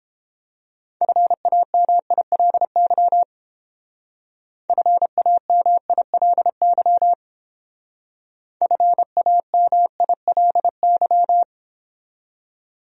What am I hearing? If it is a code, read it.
Morse code, heard as a keyed tone, about 28 wpm, 700 Hz. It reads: FAMILY FAMILY FAMILY